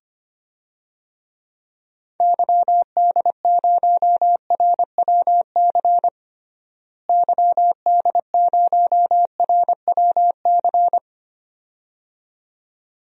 Morse code, heard as a keyed tone, about 25 wpm, 700 Hz. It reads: YD0RWC YD0RWC
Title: DXCC-100 Call Signs - 6 Characters - Repeated 2x - 25wpm